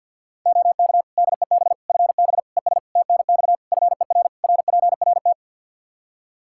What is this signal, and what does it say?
Morse code, heard as a keyed tone, about 48 wpm, 700 Hz. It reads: OK DEB FB ES TNX FER RPRT
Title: Practice QSO - Exchange 2 - Info - 50wpm